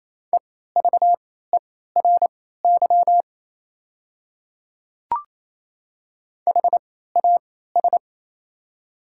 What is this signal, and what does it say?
Morse code, heard as a keyed tone, about 27 wpm, spaced as for 16 wpm, 690 Hz. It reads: EVERY HAS
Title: 3x Character Spacing - Top 200 Words - 28wpm